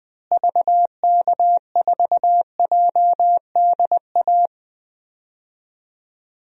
Morse code, heard as a keyed tone, about 21 wpm, 700 Hz. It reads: VK4JDA
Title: DXCC-100 Call Signs - 6 Characters - Repeated 3x - 20wpm